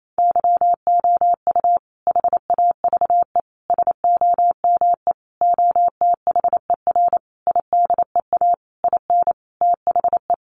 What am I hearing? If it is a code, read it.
Morse code, heard as a keyed tone, about 28 wpm, 700 Hz. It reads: YOU HAVE SOME OTHER IDEA IN THE
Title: Sentences from Top 1300 Words 28wpm